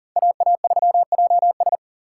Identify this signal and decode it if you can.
Morse code, heard as a keyed tone, about 40 wpm, 700 Hz. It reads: AA3JS
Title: USA Call Signs - 5 Characters - 6x Character Spacing - 40wpm